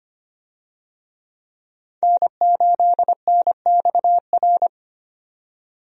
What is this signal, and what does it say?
Morse code, heard as a keyed tone, about 25 wpm, 700 Hz. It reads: N8NXR
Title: USA Call Signs - 5 Characters - 25wpm